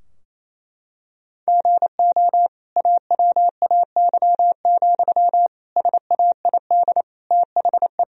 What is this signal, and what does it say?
Morse code, heard as a keyed tone, about 28 wpm, 700 Hz. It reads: GO AWAY, SAID THE
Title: Sentences from Top 1000 Words - Review Part 2 - 28wpm